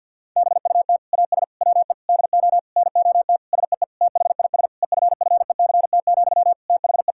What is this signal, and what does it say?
Morse code, heard as a keyed tone, about 51 wpm, 700 Hz. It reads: BUT AS WE DO NOT SEE THIS EFFECT, THE
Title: Sentences from Top 1600 Words 50wpm